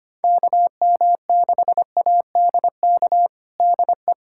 Morse code, heard as a keyed tone, about 25 wpm, 700 Hz. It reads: KM6ADK DE
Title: USA Callsign DE Callsign - 6 Characters - 25wpm